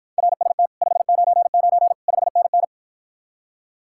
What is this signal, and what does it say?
Morse code, heard as a keyed tone, about 54 wpm, 700 Hz. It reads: RST 599 5NN